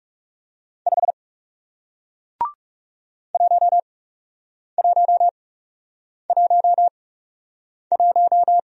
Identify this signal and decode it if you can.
Morse code, heard as a keyed tone, about 40 wpm, spaced as for 8 wpm, 700 Hz. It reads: 5 1111